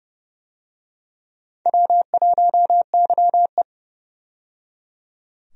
Morse code, heard as a keyed tone, about 30 wpm, 700 Hz. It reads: W1YE
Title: USA Call Signs - 4 Characters - 30wpm